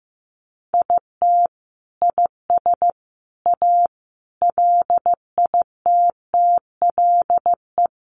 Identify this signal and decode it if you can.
Morse code, heard as a keyed tone, about 15 wpm, 700 Hz. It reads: IT IS A LITTLE
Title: Sentences from Top 1300 Words 15wpm